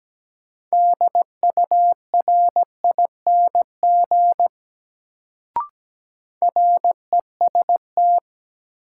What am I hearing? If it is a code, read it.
Morse code, heard as a keyed tone, about 17 wpm, 700 Hz. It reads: DURING REST
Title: Sets of 2 Words - New Word Drill - 17wpm